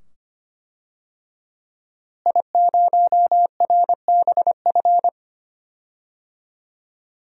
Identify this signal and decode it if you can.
Morse code, heard as a keyed tone, about 25 wpm, 700 Hz. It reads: I0RBF
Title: DXCC-100 Call Signs - 5 Characters - Repeated 2x - 25wpm